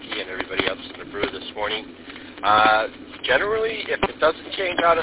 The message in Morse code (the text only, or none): none